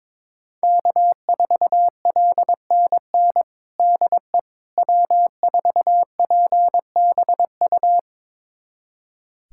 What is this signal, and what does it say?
Morse code, heard as a keyed tone, about 22 wpm, 700 Hz. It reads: K4LNN DE W4PBU
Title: USA Callsign DE Callsign - 5 Characters - 22wpm